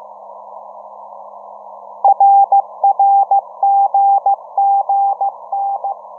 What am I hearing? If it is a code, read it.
Morse code, heard as a keyed tone, about 15 wpm, 800 Hz. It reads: RRGGN